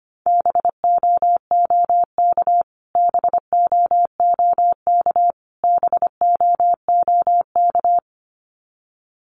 Morse code, heard as a keyed tone, about 25 wpm, 700 Hz. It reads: BOOK BOOK BOOK
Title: Top 500 Words Repeated 25wpm